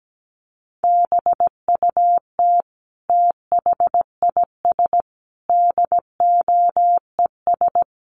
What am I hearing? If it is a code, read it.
Morse code, heard as a keyed tone, about 17 wpm, 700 Hz. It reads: BUT THIS DOES
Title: Sentences from Top 1400 Words 17wpm